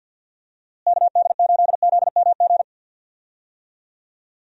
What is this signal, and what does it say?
Morse code, heard as a keyed tone, about 47 wpm, 700 Hz. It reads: KD8ZMG
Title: USA Call Signs - 6 Characters - Repeated 2x - 50wpm